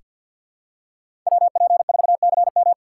Morse code, heard as a keyed tone, about 46 wpm, 700 Hz. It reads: WP4CM
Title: USA Call Signs - 5 Characters - 6x Character Spacing - 50wpm